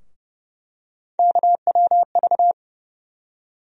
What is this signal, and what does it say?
Morse code, heard as a keyed tone, about 31 wpm, 700 Hz. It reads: KWV
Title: USA Call Sign Postfixes - 3 Letters 30wpm